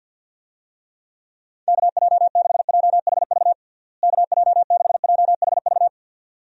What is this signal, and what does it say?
Morse code, heard as a keyed tone, about 50 wpm, 700 Hz. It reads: KJ6JHV KJ6JHV
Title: USA Call Signs - 6 Characters - Repeated 2x - 50wpm